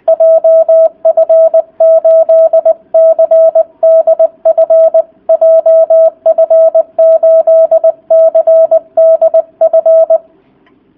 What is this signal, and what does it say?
Morse code, heard as a keyed tone, about 20 wpm, 650 Hz. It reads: JF8CDF JF8CDF